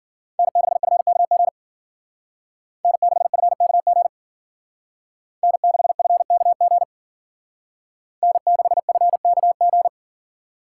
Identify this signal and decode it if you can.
Morse code, heard as a keyed tone, about 52 wpm, 700 Hz. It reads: N6FKG N6FKG N6FKG N6FKG